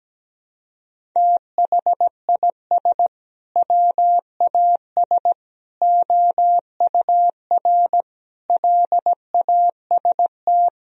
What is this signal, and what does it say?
Morse code, heard as a keyed tone, about 17 wpm, 700 Hz. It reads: THIS WAS OUR LAST
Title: Sentences from Top 1000 Words - Review Part 1 - 17wpm